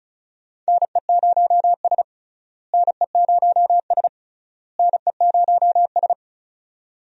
Morse code, heard as a keyed tone, about 36 wpm, 700 Hz. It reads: NE0S NE0S NE0S
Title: USA Call Signs - 4 Characters - Repeated 3x - 35wpm